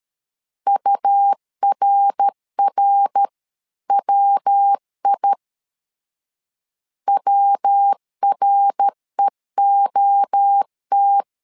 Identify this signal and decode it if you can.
Morse code, heard as a keyed tone, about 13 wpm, 800 Hz. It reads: URRWI WREOT
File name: LEZIONI / 3 MWRO / 3 MWRO 12wpm 1730